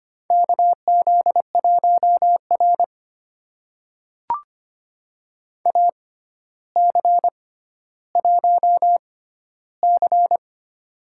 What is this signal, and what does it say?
Morse code, heard as a keyed tone, about 25 wpm, spaced as for 13 wpm, 700 Hz. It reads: KZ1R AC1C
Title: USA Call Signs - 4 Characters - 6x Character Spacing - 25wpm